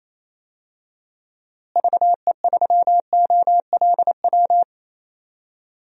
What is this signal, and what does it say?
Morse code, heard as a keyed tone, about 28 wpm, 700 Hz. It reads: VE3OLW
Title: DXCC-100 Call Signs - 6 Characters - Repeated 2x - 28wpm